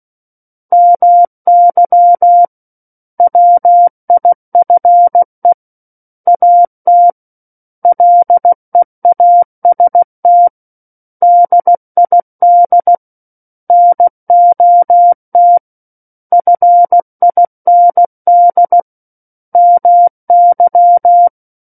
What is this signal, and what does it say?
Morse code, heard as a keyed tone, about 16 wpm, 700 Hz. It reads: MY WIFE AT LEAST DID NOT FIND MY
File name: Book One / 07-Chapter 7 - 16 WPM